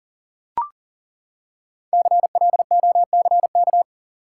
Morse code, heard as a keyed tone, about 40 wpm, 700 Hz. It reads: CLOCK
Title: Five Letter Words - 40wpm